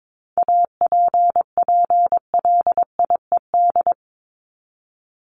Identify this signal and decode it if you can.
Morse code, heard as a keyed tone, about 22 wpm, 700 Hz. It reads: APPLIED
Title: ED Suffix - Top 100 Words with Base Words 22wpm